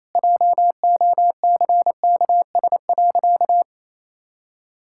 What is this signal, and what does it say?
Morse code, heard as a keyed tone, about 28 wpm, 700 Hz. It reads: JOCKS.
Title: Pangram Challenge v2 - 28wpm